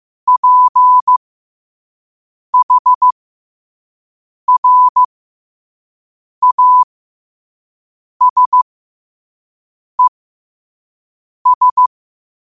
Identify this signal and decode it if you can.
Morse code, heard as a keyed tone, about 15 wpm, spaced as for 5 wpm, 1000 Hz. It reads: PHRASES
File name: qotdfiles / 5wpm